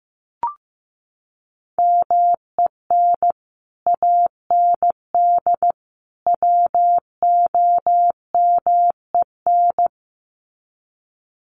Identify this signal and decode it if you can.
Morse code, heard as a keyed tone, about 15 wpm, 700 Hz. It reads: MEN AND WOMEN
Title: Sets of 3 Words 15wpm